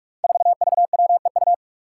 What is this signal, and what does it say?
Morse code, heard as a keyed tone, about 46 wpm, 700 Hz. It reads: 4VWEU
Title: Five Challenging Letter-Number Combinations 45wpm